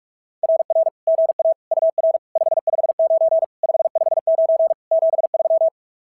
Morse code, heard as a keyed tone, about 46 wpm, 690 Hz. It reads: RR GA UR 559 559 73
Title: SOTA QSO Practice 45wpm